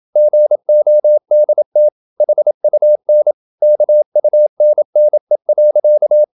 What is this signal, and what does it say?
Morse code, heard as a keyed tone, about 27 wpm, 600 Hz. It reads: GODT HUN KUNNE.
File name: Ferie QRQ 27wpm